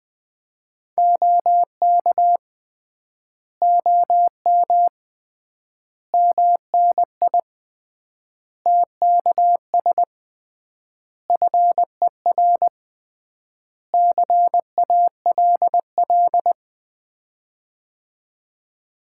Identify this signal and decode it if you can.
Morse code, heard as a keyed tone, about 21 wpm, 700 Hz. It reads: OK OM MNI TKS FER CALL
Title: Common QSO Phrases - 3x Word-Spacing - 20wpm